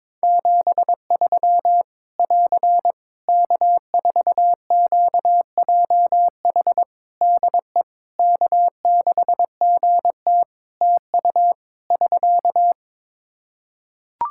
Morse code, heard as a keyed tone, about 22 wpm, 700 Hz. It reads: 73 <AR> K4QJH DE K6GT TU <SK>
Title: Practice QSO - Exchange 4 - Ending - 22wpm